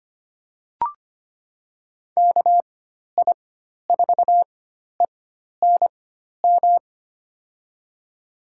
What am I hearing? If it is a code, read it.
Morse code, heard as a keyed tone, about 25 wpm, spaced as for 12 wpm, 700 Hz. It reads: KI4ENM
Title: USA Call Signs - 6 Characters - 4x Character Spacing - 25wpm